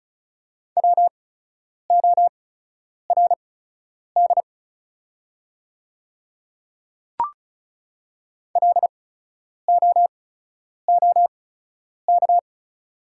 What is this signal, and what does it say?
Morse code, heard as a keyed tone, about 35 wpm, spaced as for 10 wpm, 700 Hz. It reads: WORD LOOK
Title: 8x Character Spacing - Top 100 Words - 35wpm